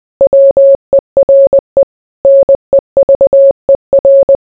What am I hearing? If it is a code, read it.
Morse code, heard as a keyed tone, about 20 wpm, 550 Hz. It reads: WERE NEVER